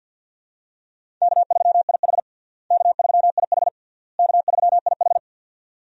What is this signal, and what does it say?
Morse code, heard as a keyed tone, about 52 wpm, 700 Hz. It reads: K3IH K3IH K3IH
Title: USA Call Signs - 4 Characters - Repeated 3x - 50wpm